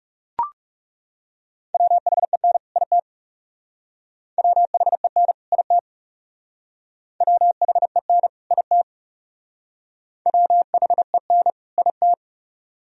Morse code, heard as a keyed tone, about 38 wpm, 700 Hz. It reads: WHEN IT WHEN IT WHEN IT WHEN IT